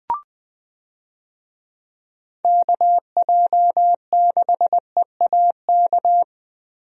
Morse code, heard as a keyed tone, about 20 wpm, 700 Hz. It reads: KJ6EAK
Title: USA Call Signs - 6 Characters - Repeated 2x - 20wpm